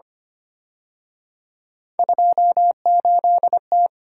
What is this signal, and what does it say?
Morse code, heard as a keyed tone, about 29 wpm, 700 Hz. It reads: 28T
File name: Speed-Racing - Three Letter-Number Combinations - 25wpm